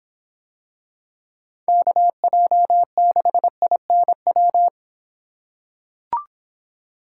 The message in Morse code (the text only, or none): KJ6INW